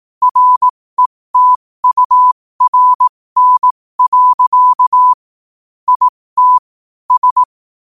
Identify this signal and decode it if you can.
Morse code, heard as a keyed tone, about 18 wpm, spaced as for 15 wpm, 1000 Hz. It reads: RETURN. IT S
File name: qotdfiles / 15wpm